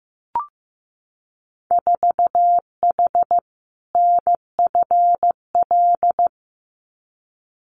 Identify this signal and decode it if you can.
Morse code, heard as a keyed tone, about 15 wpm, 700 Hz. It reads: 4H NFL
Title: Winter Field Day Exchanges Encore 15wpm